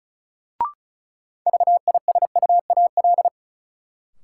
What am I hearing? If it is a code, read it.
Morse code, heard as a keyed tone, about 36 wpm, 700 Hz. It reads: VISUAL
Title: Sentences from Top 1000 Words - Review Part 6 - New Word Drill - 35wpm